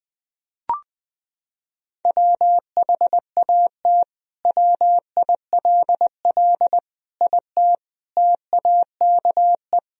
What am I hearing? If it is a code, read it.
Morse code, heard as a keyed tone, about 20 wpm, 700 Hz. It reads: WHAT WILL IT TAKE
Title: Sentences from Top 1000 Words - Review Part 3 - 20wpm